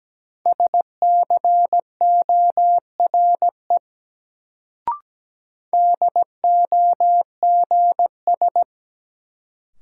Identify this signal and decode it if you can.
Morse code, heard as a keyed tone, about 17 wpm, 700 Hz. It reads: SCORE DOGS
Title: Sentences from Top 1600 Words - New Word Drill Alt - 17wpm